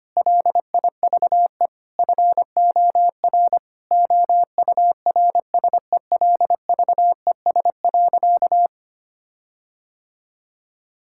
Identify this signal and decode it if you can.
Morse code, heard as a keyed tone, about 25 wpm, 700 Hz. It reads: LIVE FOR OURSELVES.
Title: Sentences from Top 1600 Words 25wpm